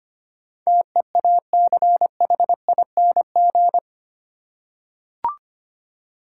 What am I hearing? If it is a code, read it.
Morse code, heard as a keyed tone, about 25 wpm, 700 Hz. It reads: TEACHING
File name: Eight Letter Words - 25wpm